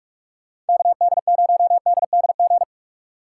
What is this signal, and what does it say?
Morse code, heard as a keyed tone, about 56 wpm, 700 Hz. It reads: KD0DDG